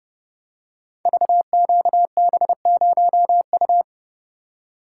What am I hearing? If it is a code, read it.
Morse code, heard as a keyed tone, about 29 wpm, 700 Hz. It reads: VQB0U
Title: Code Groups - 5 Letter:Numbers - 30wpm